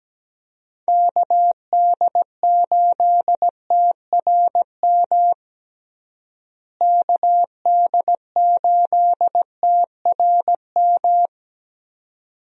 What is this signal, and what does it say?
Morse code, heard as a keyed tone, about 17 wpm, 700 Hz. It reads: KD8TRM KD8TRM
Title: USA Call Signs - 6 Characters - Repeated 2x - 17wpm